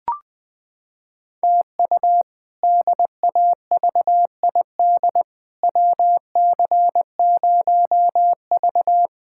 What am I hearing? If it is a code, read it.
Morse code, heard as a keyed tone, about 20 wpm, 700 Hz. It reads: TU DAVID WC0V
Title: K1USN SST QSO Practice - 20wpm